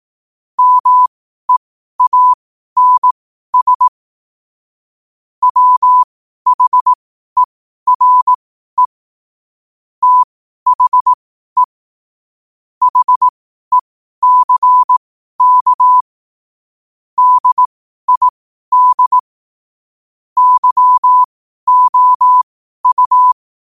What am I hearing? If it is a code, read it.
Morse code, heard as a keyed tone, about 18 wpm, spaced as for 12 wpm, 1000 Hz. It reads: MEANS WHERE THE HECK DID YOU
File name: qotdfiles / 12wpm